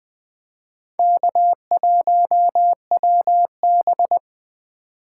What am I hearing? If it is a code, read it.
Morse code, heard as a keyed tone, about 25 wpm, 700 Hz. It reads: K1WB
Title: Speed Racing - USA 1x2 Call Signs 20wpm